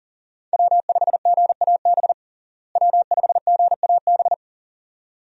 Morse code, heard as a keyed tone, about 37 wpm, 700 Hz. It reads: W5GAB W5GAB
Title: USA Call Signs - 5 Characters - Repeated 2x - 40wpm